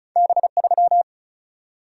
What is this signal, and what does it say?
Morse code, heard as a keyed tone, about 36 wpm, 700 Hz. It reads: B3
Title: Single-Letter Single-Number 35wpm